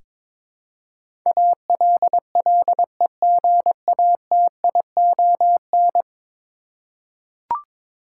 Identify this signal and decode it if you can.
Morse code, heard as a keyed tone, about 21 wpm, 700 Hz. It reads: ALLEGATION
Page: morse-code-ninja-practice-22wpm